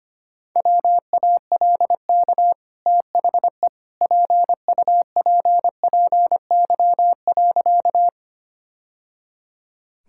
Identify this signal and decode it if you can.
Morse code, heard as a keyed tone, about 25 wpm, 690 Hz. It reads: WALK THE PUPPY.
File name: Easy Sentences with Hints - 25wpm